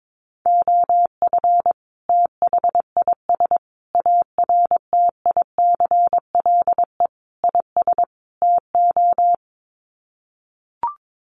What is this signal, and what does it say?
Morse code, heard as a keyed tone, about 22 wpm, 700 Hz. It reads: OF THIS ARTICLE IS TO
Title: Sets of 5 Words Encore2 22wpm